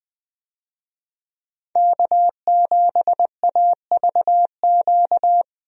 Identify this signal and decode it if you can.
Morse code, heard as a keyed tone, about 20 wpm, 700 Hz. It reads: K7AVQ
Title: USA Call Signs - 5 Characters - 20wpm